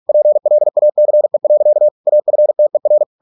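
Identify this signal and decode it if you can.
Morse code, heard as a keyed tone, about 46 wpm, 600 Hz. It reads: PLACE. AFTER
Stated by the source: Morse practice at 45 wpm, the novel Never by Ken Follett in English, from CW med Gnister.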